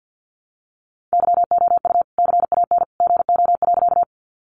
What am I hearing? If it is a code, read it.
Morse code, heard as a keyed tone, about 50 wpm, 700 Hz. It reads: YOU CAN GO.
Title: Sentences from Top 100 Words 50wpm